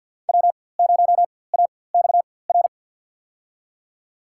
Morse code, heard as a keyed tone, about 53 wpm, spaced as for 24 wpm, 700 Hz. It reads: W0AXR
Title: USA Call Signs - 4x Character Spacing - 50wpm